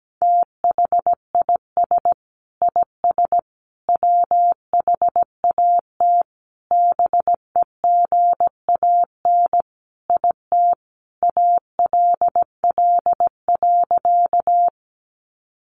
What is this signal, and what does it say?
Morse code, heard as a keyed tone, about 17 wpm, 700 Hz. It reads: THIS IS WHAT BEGAN IT ALL.
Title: Sentences from Top 300 Words 17wpm